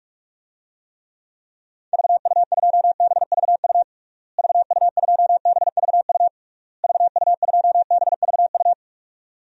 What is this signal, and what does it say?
Morse code, heard as a keyed tone, about 41 wpm, 700 Hz. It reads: VU2BVU VU2BVU VU2BVU
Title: DXCC-100 Call Signs - 6 Characters - Repeated 3x - 45wpm